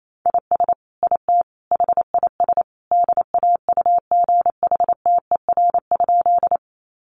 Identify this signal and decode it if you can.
Morse code, heard as a keyed tone, about 28 wpm, 700 Hz. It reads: IS IT HIS DAUGHTER?
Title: Sentences from Top 1500 Words 28wpm